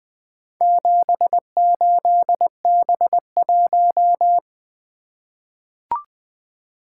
Morse code, heard as a keyed tone, about 20 wpm, 700 Hz. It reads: 78B1
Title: Four Letter-Number Combinations - 20wpm